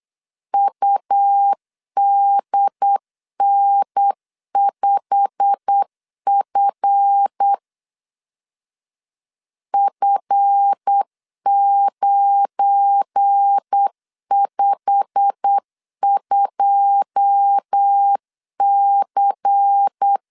Morse code, heard as a keyed tone, about 8 wpm, 800 Hz. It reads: UDN5F F952C
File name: LEZIONI / 9 2 8wpm 3437